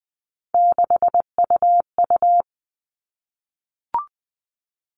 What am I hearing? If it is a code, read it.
Morse code, heard as a keyed tone, about 20 wpm, 700 Hz. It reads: 6UU
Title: All Cut Numbers - 3-Digit 20wpm